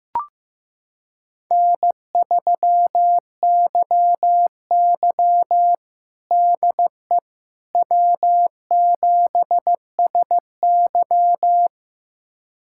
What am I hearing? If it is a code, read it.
Morse code, heard as a keyed tone, about 15 wpm, 700 Hz. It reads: N3YY DE W7SY
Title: USA Callsign DE Callsign - 4 Characters - 15wpm